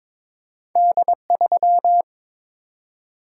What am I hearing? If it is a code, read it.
Morse code, heard as a keyed tone, about 22 wpm, 700 Hz. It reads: D3
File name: practice / 22wpm / Two Letter-Number Combinations - 22wpm